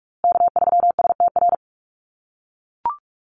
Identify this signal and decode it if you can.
Morse code, heard as a keyed tone, about 44 wpm, 700 Hz. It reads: K3STR
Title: USA Call Signs Repeated 2x 45wpm